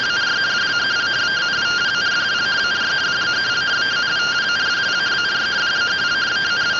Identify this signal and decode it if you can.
Two-tone teleprinter signal, about 170 Hz shift.